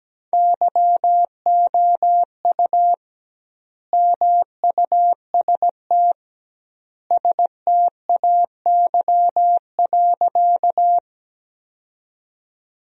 Morse code, heard as a keyed tone, about 17 wpm, 700 Hz. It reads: YOU MUST STAY.
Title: 2x Word-Spacing - Sentences from Top 500 Words - 17wpm